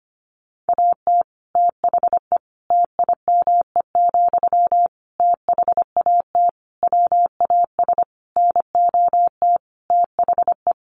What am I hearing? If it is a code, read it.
Morse code, heard as a keyed tone, about 25 wpm, 700 Hz. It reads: AT THE TIME, THAT WAS NOT THE
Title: Sentences from Top 1400 Words 25wpm